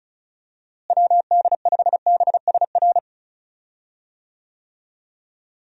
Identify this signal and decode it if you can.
Morse code, heard as a keyed tone, about 34 wpm, 700 Hz. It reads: WD5BSR
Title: USA Call Signs - 6 Characters - Repeated 3x - 35wpm